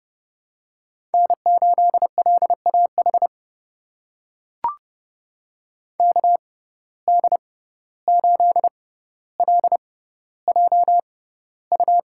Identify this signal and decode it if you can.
Morse code, heard as a keyed tone, about 30 wpm, spaced as for 15 wpm, 700 Hz. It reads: N8LAH KD8LJU
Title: USA Call Signs - 6x Character Spacing - 30wpm